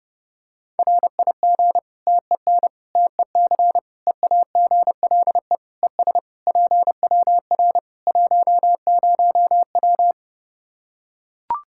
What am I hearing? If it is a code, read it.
Morse code, heard as a keyed tone, about 30 wpm, 700 Hz. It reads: RIG TEN TEC EAGLE ES PWR 10W
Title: K1USN SST QSO Practice 30wpm